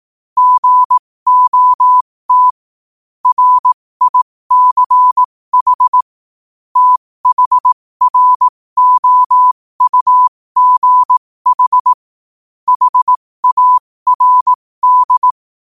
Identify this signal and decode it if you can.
Morse code, heard as a keyed tone, about 18 wpm, spaced as for 15 wpm, 1000 Hz. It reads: GOT RICH THROUGH HARD
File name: qotdfiles / 15wpm